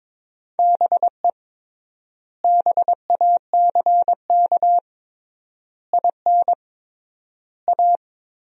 Morse code, heard as a keyed tone, about 22 wpm, 700 Hz. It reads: BE BACK IN A
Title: 3x Word-Spacing - Sentences from Top 500 Words - 22wpm